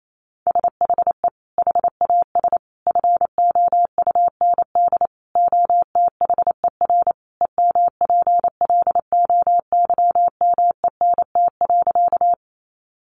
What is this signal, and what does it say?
Morse code, heard as a keyed tone, about 28 wpm, 700 Hz. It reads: SHE HAS FOUND OTHER EMPLOYMENT.
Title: Sentences from Top 1500 Words 28wpm